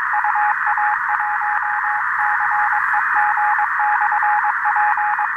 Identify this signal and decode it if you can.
Morse code, heard as a keyed tone, about 22 wpm, 840 Hz, with radio noise. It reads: UA1CEG/P
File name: attachments / 13.03.2015 RV3GM QSO UA1CEG P QRPP